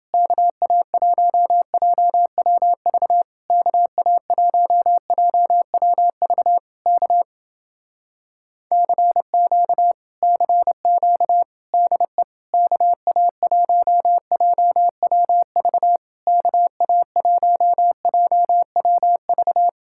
Morse code, heard as a keyed tone, about 29 wpm, 700 Hz. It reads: KA1JWV KA1JWV K CQ CQ DE KA1JWV KA1JWV